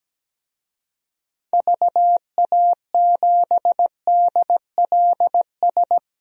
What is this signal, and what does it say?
Morse code, heard as a keyed tone, about 17 wpm, 700 Hz. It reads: VA7DLS
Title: DXCC-100 Call Signs - 6 Characters - 17wpm